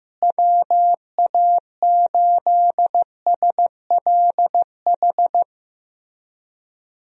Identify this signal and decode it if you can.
Morse code, heard as a keyed tone, about 15 wpm, 700 Hz. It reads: WA8SLH
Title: USA Call Signs - 6 Characters - Repeated 3x - 15wpm